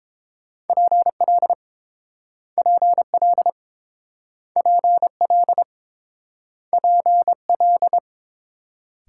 Morse code, heard as a keyed tone, about 30 wpm, 700 Hz. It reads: PL PL PL PL